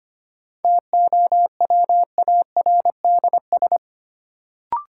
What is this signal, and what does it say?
Morse code, heard as a keyed tone, about 25 wpm, 700 Hz. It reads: TOWARDS
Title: Sentences from Top 1000 Words - Review Part 2 - New Word Drill - 25wpm